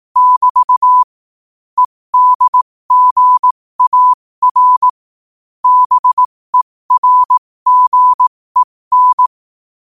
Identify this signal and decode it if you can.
Morse code, heard as a keyed tone, about 18 wpm, spaced as for 15 wpm, 1000 Hz. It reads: <BT> EDGAR BERGEN
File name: qotdfiles / 15wpm